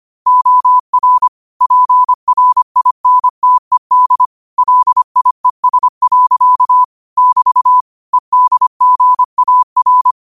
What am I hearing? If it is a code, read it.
Morse code, heard as a keyed tone, about 25 wpm, 1000 Hz. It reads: OR PRINTED LIES. <BT> EDGAR
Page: New quotes every day in morse code at 25 Words per minute.